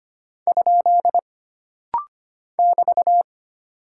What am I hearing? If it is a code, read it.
Morse code, heard as a keyed tone, about 25 wpm, 700 Hz. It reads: ? <BT>
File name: Single Punctuation-Prosign - Rapid-Fire - 25wpm